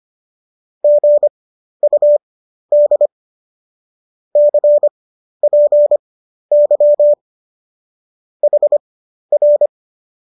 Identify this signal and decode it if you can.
Morse code, heard as a keyed tone, about 25 wpm, spaced as for 12 wpm, 590 Hz. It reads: GUD CPY HR
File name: qso009 12